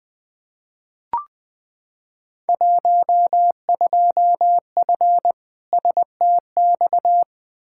Morse code, heard as a keyed tone, about 20 wpm, 700 Hz. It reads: 12F STX
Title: Field Day Exchanges Encore - 20wpm